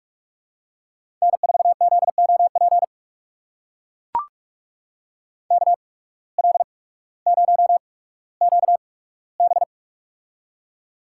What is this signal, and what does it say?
Morse code, heard as a keyed tone, about 44 wpm, spaced as for 19 wpm, 700 Hz. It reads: N4ZOP KL0QB
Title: USA Call Signs - 8x Character Spacing - 45wpm